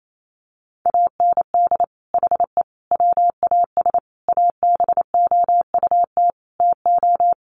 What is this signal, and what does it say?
Morse code, heard as a keyed tone, about 28 wpm, 700 Hz. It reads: AND HE WAS ABOUT TO
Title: Sentences from Top 1300 Words 28wpm